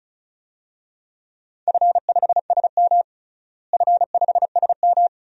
Morse code, heard as a keyed tone, about 36 wpm, 700 Hz. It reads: F5SM F5SM
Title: DXCC-100 Call Signs - 4 Characters - Repeated 2x - 35wpm